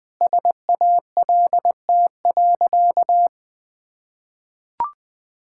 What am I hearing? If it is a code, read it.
Morse code, heard as a keyed tone, about 20 wpm, 700 Hz. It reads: SALT.
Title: Sentences with 8-Word Phrases - Spoken Beginning - 20wpm